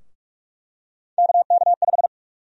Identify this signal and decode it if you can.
Morse code, heard as a keyed tone, about 46 wpm, 700 Hz. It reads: KK5
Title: USA Call Sign Prefixes - 45wpm